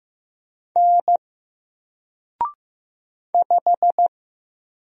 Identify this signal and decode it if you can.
Morse code, heard as a keyed tone, about 15 wpm, 700 Hz. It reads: N 5
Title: Single Letter-Number Alt 15wpm